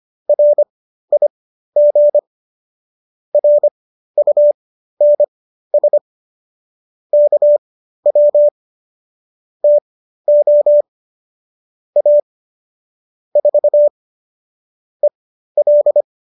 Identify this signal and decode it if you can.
Morse code, heard as a keyed tone, about 25 wpm, spaced as for 13 wpm, 590 Hz. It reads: RIG RUNS KW TO A 4 EL